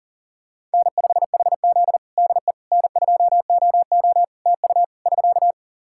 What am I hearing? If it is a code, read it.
Morse code, heard as a keyed tone, about 40 wpm, 700 Hz. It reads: N5HZ DE N2OO TU <SK>
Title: Practice QSO - Exchange 4 - Ending - 40wpm